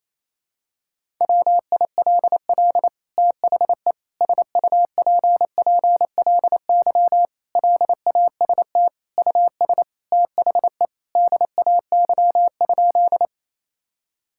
Morse code, heard as a keyed tone, about 28 wpm, 700 Hz. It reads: WILL THE SUPPLY LAST US THE DAY?
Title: Sentences from Top 1000 Words - Review Part 5 - 28wpm